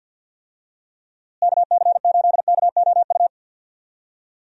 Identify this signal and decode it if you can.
Morse code, heard as a keyed tone, about 48 wpm, 700 Hz. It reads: KC8KOU
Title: USA Call Signs - 6 Characters - Repeated 3x - 50wpm